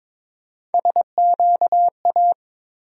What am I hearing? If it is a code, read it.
Morse code, heard as a keyed tone, about 22 wpm, 700 Hz. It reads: SQA
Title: USA Call Sign Postfixes - 3 Letters 22wpm